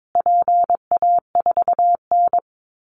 Morse code, heard as a keyed tone, about 22 wpm, 700 Hz. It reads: PA4N
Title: Call Signs Repeated 2x - 22wpm